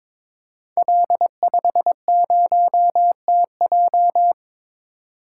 Morse code, heard as a keyed tone, about 22 wpm, 700 Hz. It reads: L50TJ
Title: Code Groups - 5 Letter:Numbers - 22wpm